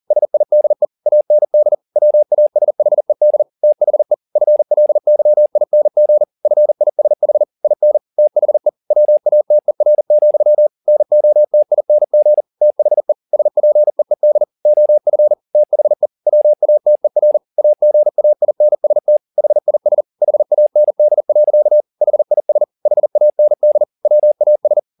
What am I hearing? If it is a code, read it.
Morse code, heard as a keyed tone, about 40 wpm, 600 Hz. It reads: SIDE AND WASHED THE FLYING FISH IN THE WATER, NOTING THE SPEED OF THE WATER AGAINST HIS HAND. HIS HAND WAS
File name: the old man and the sea 2 40wpm